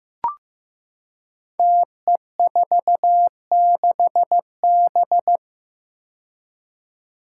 Morse code, heard as a keyed tone, about 15 wpm, 700 Hz. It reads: TE46B
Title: Five Challenging Letter-Number Combinations 15wpm